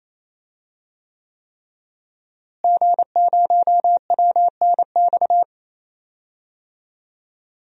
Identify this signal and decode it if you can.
Morse code, heard as a keyed tone, about 27 wpm, 700 Hz. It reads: G0WNX
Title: DXCC-100 Call Signs - 5 Characters - Repeated 3x - 28wpm